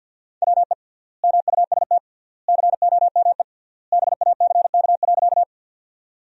Morse code, heard as a keyed tone, about 53 wpm, 700 Hz. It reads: WE MUST COME BACK.
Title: 3x Word-Spacing - Sentences from Top 200 Words - 50wpm